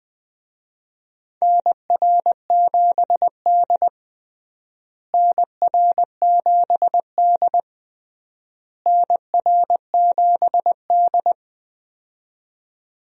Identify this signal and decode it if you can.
Morse code, heard as a keyed tone, about 20 wpm, 700 Hz. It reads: NR7D NR7D NR7D
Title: USA Call Signs - 4 Characters - Repeated 3x - 20wpm